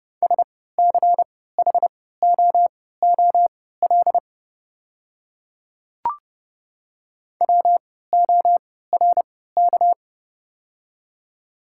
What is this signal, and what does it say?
Morse code, heard as a keyed tone, about 32 wpm, spaced as for 17 wpm, 700 Hz. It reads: SCHOOL WORK
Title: 3x Character Spacing - Top 300 Words - 30wpm